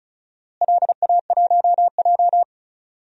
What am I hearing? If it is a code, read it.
Morse code, heard as a keyed tone, about 36 wpm, 700 Hz. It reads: LA1J
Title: DXCC-100 Call Signs - 4 Characters - Repeated 3x - 35wpm